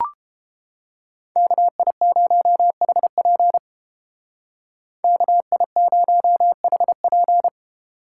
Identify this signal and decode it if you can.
Morse code, heard as a keyed tone, about 29 wpm, 700 Hz. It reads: KI0HP KI0HP